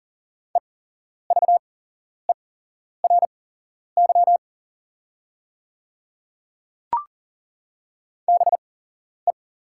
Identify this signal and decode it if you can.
Morse code, heard as a keyed tone, about 38 wpm, spaced as for 11 wpm, 690 Hz. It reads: EVERY BE
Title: 8x Character Spacing - Top 300 Words - 40wpm